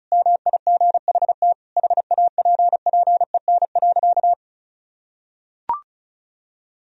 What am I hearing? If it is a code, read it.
Morse code, heard as a keyed tone, about 34 wpm, 700 Hz. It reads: MIGHT HAPPEN.
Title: Sentences from Top 400 Words and Abbreviations 35wpm